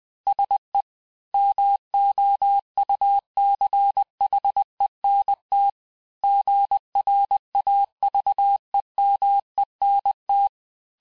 Morse code, heard as a keyed tone, about 20 wpm, 800 Hz. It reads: SE MOUCHENT GRAVEMENT
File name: cours-du-mercredi / audios / cours-858-texte-clais-vitesse-19mpm